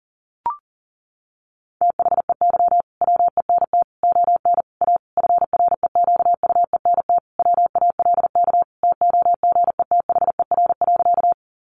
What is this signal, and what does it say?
Morse code, heard as a keyed tone, about 40 wpm, 700 Hz. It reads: THEY WENT ON A FREQUENT WALK TOGETHER.
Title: Sentences from Top 1400 Words 40wpm